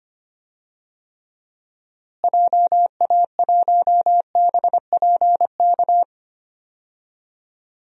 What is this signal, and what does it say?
Morse code, heard as a keyed tone, about 25 wpm, 700 Hz. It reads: JA1BPK
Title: DXCC-100 Call Signs - 6 Characters - Repeated 3x - 25wpm